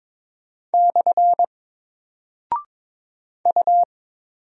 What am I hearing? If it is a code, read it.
Morse code, heard as a keyed tone, about 22 wpm, 700 Hz. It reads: / U
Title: Single Character (Letter:Number:Punctuation:Prosign) 22wpm